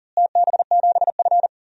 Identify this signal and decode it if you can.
Morse code, heard as a keyed tone, about 40 wpm, 700 Hz. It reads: TB7F
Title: DXCC-100 Call Signs - 4 Characters - Repeated 2x - 40wpm